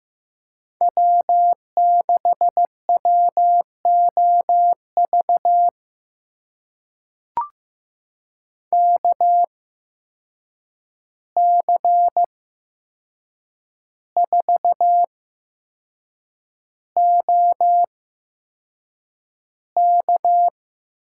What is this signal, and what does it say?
Morse code, heard as a keyed tone, about 15 wpm, spaced as for 6 wpm, 700 Hz. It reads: W6WOV KC4OK
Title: USA Call Signs - 5 Characters - 8x Character Spacing - 15wpm